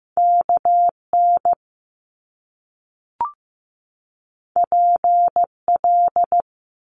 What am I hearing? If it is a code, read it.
Morse code, heard as a keyed tone, about 15 wpm, 700 Hz. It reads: KN PL
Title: Top Two Letter Combinations in Words 15wpm